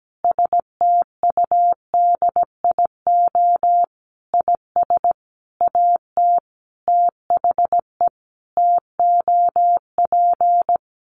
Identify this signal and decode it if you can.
Morse code, heard as a keyed tone, about 17 wpm, 700 Hz. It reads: STUDIO IS AT THE TOP
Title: Sentences from Top 1200 Words 17wpm